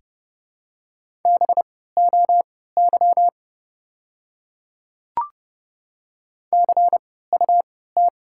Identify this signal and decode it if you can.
Morse code, heard as a keyed tone, about 32 wpm, spaced as for 17 wpm, 700 Hz. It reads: BOY CUT
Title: 3x Character Spacing - Top 500 Words - 30wpm